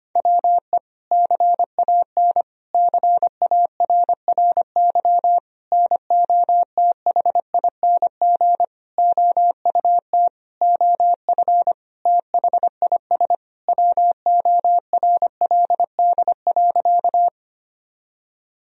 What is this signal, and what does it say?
Morse code, heard as a keyed tone, about 25 wpm, 700 Hz. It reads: WE CAN CARRY NOTHING OUT OF THIS WORLD.
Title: Sentences from Top 1000 Words - Review Part 1 - 25wpm